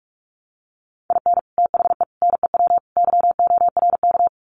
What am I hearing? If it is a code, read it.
Morse code, heard as a keyed tone, about 44 wpm, 700 Hz. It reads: IN THE NEW YORK
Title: Sets of 4 Words 45wpm